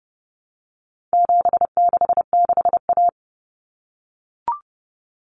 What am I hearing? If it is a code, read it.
Morse code, heard as a keyed tone, about 30 wpm, 700 Hz. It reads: 766A
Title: Common Cut Numbers - 4-Digit 30wpm